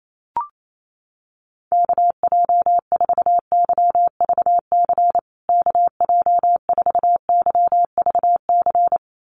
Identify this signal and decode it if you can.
Morse code, heard as a keyed tone, about 28 wpm, 700 Hz. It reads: KJ4YVC KJ4YVC
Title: USA Call Signs Repeated 2x 28wpm